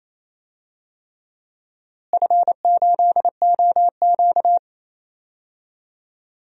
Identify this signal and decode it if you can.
Morse code, heard as a keyed tone, about 28 wpm, 700 Hz. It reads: F8OQ
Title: DXCC-100 Call Signs - 4 Characters - Repeated 3x - 28wpm